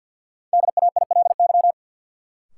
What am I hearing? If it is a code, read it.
Morse code, heard as a keyed tone, about 51 wpm, 700 Hz. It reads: DAILY
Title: Sentences from Top 1100 Words - New Word Drill Alt - 50wpm